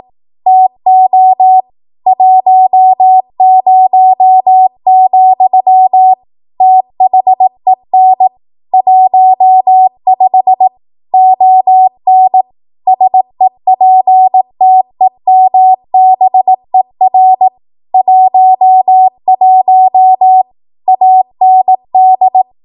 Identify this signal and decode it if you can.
Morse code, heard as a keyed tone, about 18 wpm, 750 Hz. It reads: TO 10, THEN 15 ON SEPTEMBER 11 AND
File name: Archive / 18 WPM / 18WPM